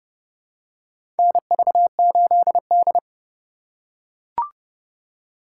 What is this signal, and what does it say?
Morse code, heard as a keyed tone, about 30 wpm, 700 Hz. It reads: NV8D
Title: USA Call Signs - 4 Characters - 2x Character Spacing - 30wpm